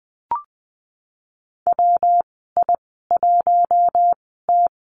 Morse code, heard as a keyed tone, about 20 wpm, spaced as for 15 wpm, 700 Hz. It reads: WI1T
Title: 147 Lesson 24g - Letter y - Call Signs 20wpm